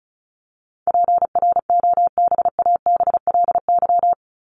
Morse code, heard as a keyed tone, about 34 wpm, 700 Hz. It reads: PROBABLY
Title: LY Suffix - Top 100 Words with Base Words 35wpm